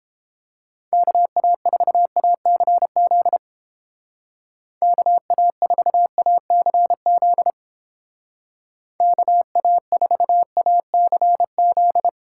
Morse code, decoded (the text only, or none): KA4ACZ KA4ACZ KA4ACZ